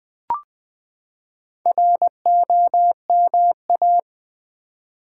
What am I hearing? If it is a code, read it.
Morse code, heard as a keyed tone, about 20 wpm, 700 Hz. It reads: ROMA
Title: Build A Word - 4 Letters At A Time - 8-Letter Words - 20wpm